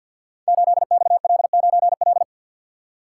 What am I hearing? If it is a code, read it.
Morse code, heard as a keyed tone, about 50 wpm, 690 Hz. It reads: 8XL9L
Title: Code Groups - 5 Letter:Numbers - 50wpm